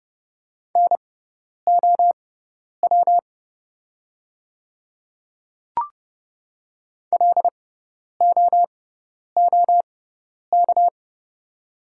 Morse code, heard as a keyed tone, about 30 wpm, spaced as for 10 wpm, 690 Hz. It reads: NOW LOOK
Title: 6x Character Spacing - Top 200 Words - 30wpm